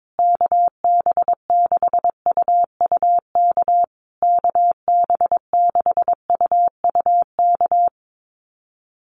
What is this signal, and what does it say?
Morse code, heard as a keyed tone, about 22 wpm, 700 Hz. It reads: KB6UUK KB6UUK
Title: USA Call Signs Repeated 3x 22wpm